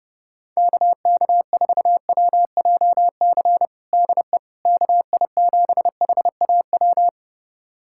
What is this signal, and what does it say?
Morse code, heard as a keyed tone, about 32 wpm, 700 Hz. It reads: KK4WJC DE KI7HAW
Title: USA Callsign DE Callsign - 6 Characters - 30wpm